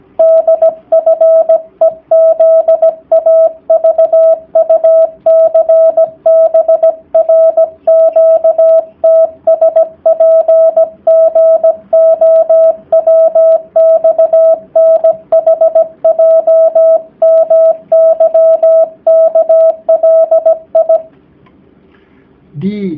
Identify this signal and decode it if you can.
Morse code, heard as a keyed tone, about 17 wpm, 650 Hz. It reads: DFEZAVUCBRQTSPGOWXNHJMYKLI
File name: jh8cbh / 2019cw / sp16- w